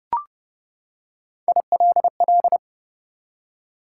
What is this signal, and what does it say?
Morse code, heard as a keyed tone, about 30 wpm, 700 Hz. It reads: ILL
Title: Three Letter Words - 30wpm